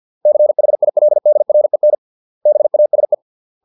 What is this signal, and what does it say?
Morse code, heard as a keyed tone, about 50 wpm, 600 Hz. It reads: CHILDREN BASE